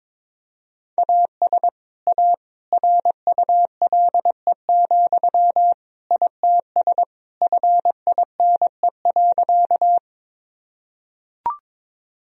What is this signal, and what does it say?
Morse code, heard as a keyed tone, about 22 wpm, 700 Hz. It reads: AS A RULE, ITS FINE.
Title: Sentences from Top 1000 Words - Review Part 2 - 22wpm